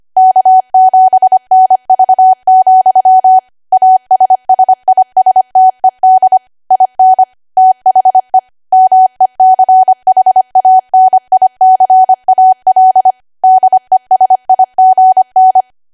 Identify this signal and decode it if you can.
Morse code, heard as a keyed tone, about 25 wpm, 750 Hz. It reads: K7NV, ASSISTED IN THE MECHANICAL DESIGN
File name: Archive / 25 WPM / 25WPM